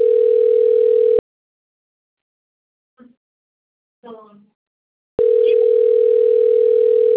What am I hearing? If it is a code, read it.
Ringback tone (North America), 440 + 480 Hz.